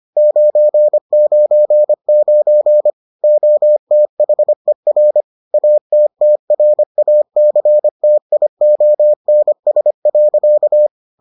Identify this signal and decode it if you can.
Morse code, heard as a keyed tone, about 25 wpm, 600 Hz. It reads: 999 OTHER ATTRACTIONS.